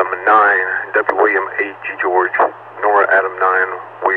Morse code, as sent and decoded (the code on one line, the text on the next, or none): none